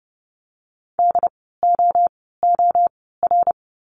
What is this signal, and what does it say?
Morse code, heard as a keyed tone, about 30 wpm, 700 Hz. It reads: D O O R
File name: Lesson 16f - Letter c - Words 30wpm